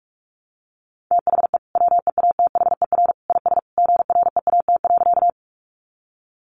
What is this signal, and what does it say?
Morse code, heard as a keyed tone, about 44 wpm, 700 Hz. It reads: THE WEATHER IS GREAT.
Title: Sentences from Top 800 Words 45wpm